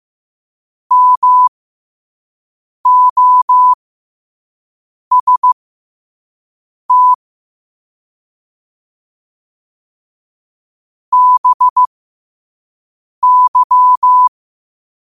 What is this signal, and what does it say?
Morse code, heard as a keyed tone, about 15 wpm, spaced as for 5 wpm, 1000 Hz. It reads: MOST BY
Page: New quotes every day in morse code at 5 Words per minute.